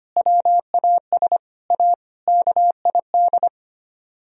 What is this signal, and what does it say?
Morse code, heard as a keyed tone, about 31 wpm, 700 Hz. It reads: WAS A KID
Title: Speed-Racing - Sets of 5 Words 25wpm